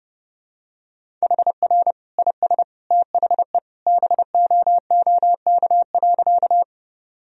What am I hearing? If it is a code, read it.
Morse code, heard as a keyed tone, about 30 wpm, 700 Hz. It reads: HR IS THE BOOK.
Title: Sentences from Top 300 Words and Abbreviations 30wpm